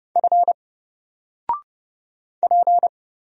Morse code, heard as a keyed tone, about 30 wpm, 700 Hz. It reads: F P
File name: Single-Letter-Number-30wpm